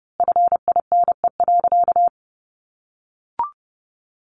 Morse code, heard as a keyed tone, about 30 wpm, 700 Hz. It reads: FINE.
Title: A Sound Of Thunder 30wpm